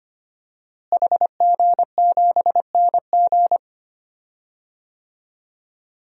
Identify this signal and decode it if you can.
Morse code, heard as a keyed tone, about 25 wpm, 700 Hz. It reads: HG7NG
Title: DXCC-100 Call Signs - 5 Characters - Repeated 2x - 25wpm